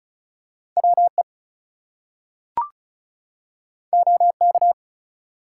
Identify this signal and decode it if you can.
Morse code, heard as a keyed tone, about 36 wpm, 700 Hz. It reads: WE OK
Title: Two Letter Words - 35wpm